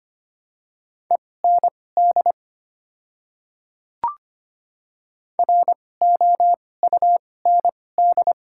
Morse code, heard as a keyed tone, about 25 wpm, spaced as for 18 wpm, 700 Hz. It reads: END ROUND
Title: 2x Character Spacing - Top 200 Words - 25wpm